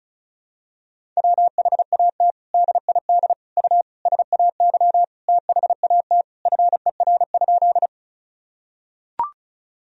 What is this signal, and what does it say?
Morse code, heard as a keyed tone, about 34 wpm, 700 Hz. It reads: WHAT DID U SAY THAT FER?
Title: Sentences from Top 200 Words and Abbreviations 35wpm